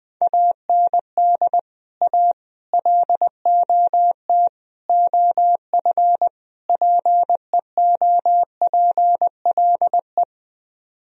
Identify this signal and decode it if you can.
Morse code, heard as a keyed tone, about 20 wpm, 700 Hz. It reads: AND A LOT OF PEOPLE
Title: Speed-Racing - Sets of 5 Words 20wpm